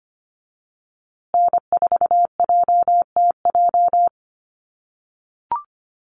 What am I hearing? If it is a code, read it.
Morse code, heard as a keyed tone, about 25 wpm, 700 Hz. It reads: N4JTJ
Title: USA Call Signs Repeated 3x 25wpm